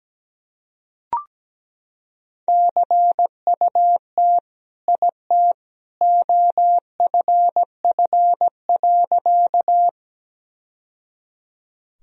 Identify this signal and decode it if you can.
Morse code, heard as a keyed tone, about 17 wpm, 700 Hz. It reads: CUT IT OFF.
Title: Sentences from Top 1000 Words - Review Part 1 - 17wpm